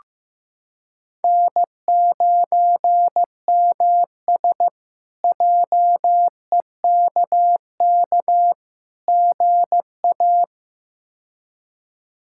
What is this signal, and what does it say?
Morse code, heard as a keyed tone, about 15 wpm, 700 Hz. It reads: N9MS JEKK GA
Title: K1USN SST QSO Practice - 15wpm